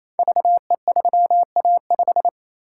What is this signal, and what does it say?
Morse code, heard as a keyed tone, about 27 wpm, 700 Hz. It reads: VE3A5
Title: Code Groups - 5 Letter:Numbers - 28wpm